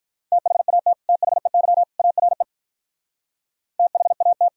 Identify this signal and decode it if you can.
Morse code, heard as a keyed tone, about 43 wpm, 700 Hz. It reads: THAT THEY ARE THAT
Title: Speed-Racing - Sets of 3 Words 35wpm